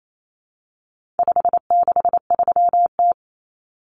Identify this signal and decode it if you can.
Morse code, heard as a keyed tone, about 28 wpm, 700 Hz. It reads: 563T
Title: Common Cut Numbers - 4-Digit 28wpm